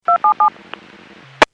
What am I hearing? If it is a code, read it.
DTMF, heard as telephone touch tones, tones of about 110 ms. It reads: 277
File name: Businesses / 00175 13-03 Businesses 451.25 WPS